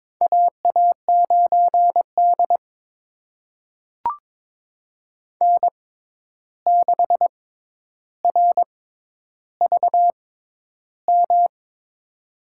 Morse code, heard as a keyed tone, about 22 wpm, spaced as for 11 wpm, 700 Hz. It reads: AA9D N6RVM
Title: USA Call Signs - 6x Character Spacing - 22wpm